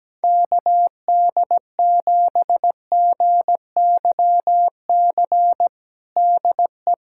Morse code, heard as a keyed tone, about 17 wpm, 700 Hz. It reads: KD7GYC DE
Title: USA Callsign DE Callsign - 6 Characters - 17wpm